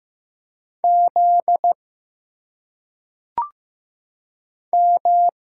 Morse code, heard as a keyed tone, about 15 wpm, 700 Hz. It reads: Z M
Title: Single Letters - Phonetic Pronunciation 15wpm